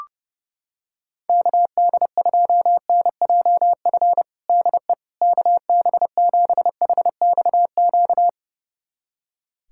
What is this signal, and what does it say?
Morse code, heard as a keyed tone, about 30 wpm, 700 Hz. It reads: KD2NJF DE KB7HXQ
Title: USA Callsign DE Callsign - 6 Characters - 30wpm